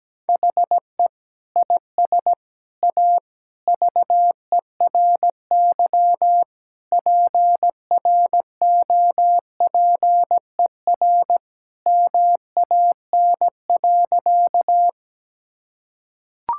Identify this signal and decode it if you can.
Morse code, heard as a keyed tone, about 17 wpm, 700 Hz. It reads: HE IS A VERY PROPER MAN.
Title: Sentences from Top 1000 Words - Review Part 5 - 17wpm